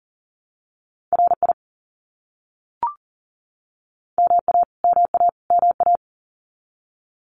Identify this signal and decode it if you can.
Morse code, heard as a keyed tone, about 40 wpm, 700 Hz. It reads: RI MA MA MA
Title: US States Repeated 3x 40wpm